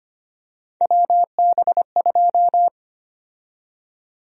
Morse code, heard as a keyed tone, about 25 wpm, 700 Hz. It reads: WB2
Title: USA Call Sign Prefixes - 25wpm